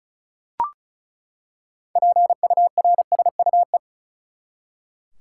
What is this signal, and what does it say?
Morse code, heard as a keyed tone, about 34 wpm, 700 Hz. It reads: PURSUE
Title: Six Letter Words - 35wpm